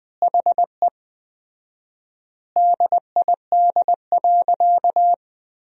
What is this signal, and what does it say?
Morse code, heard as a keyed tone, about 20 wpm, 700 Hz. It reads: HE DID.
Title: 4x Word-Spacing - Sentences from Top 500 Words - 20wpm